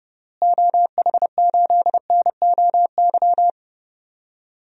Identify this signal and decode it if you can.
Morse code, heard as a keyed tone, about 30 wpm, 700 Hz. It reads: OH8NOY
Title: DXCC-100 Call Signs - 6 Characters - Repeated 3x - 30wpm